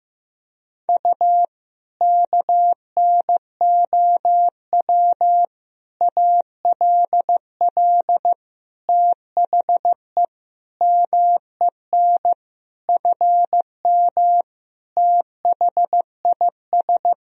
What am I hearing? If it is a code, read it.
Morse code, heard as a keyed tone, about 15 wpm, 700 Hz. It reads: U KNOW ALL THE MEN FM THIS
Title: Sentences from Top 300 Words and Abbreviations 15wpm